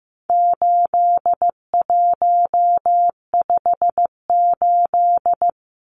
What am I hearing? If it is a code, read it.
Morse code, heard as a keyed tone, about 15 wpm, 700 Hz. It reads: 8158
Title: Numbers - 4-Digit 15wpm